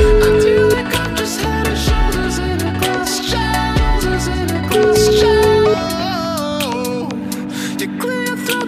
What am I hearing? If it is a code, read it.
Ringback tone (Europe, 425 Hz), with noise.